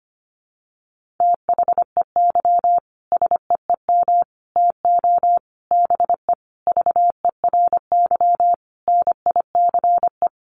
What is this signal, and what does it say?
Morse code, heard as a keyed tone, about 25 wpm, 700 Hz. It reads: THEY SEEM TO BE VERY NICE
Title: Sentences from Top 1100 Words 25wpm